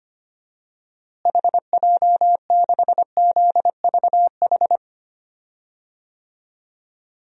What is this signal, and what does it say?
Morse code, heard as a keyed tone, about 25 wpm, 700 Hz. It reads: HJ6ZVH
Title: DXCC-100 Call Signs - 6 Characters - Repeated 3x - 25wpm